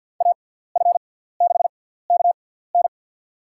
Morse code, heard as a keyed tone, about 50 wpm, spaced as for 17 wpm, 700 Hz. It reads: AF6KN
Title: USA Call Signs - 6x Character Spacing - 50wpm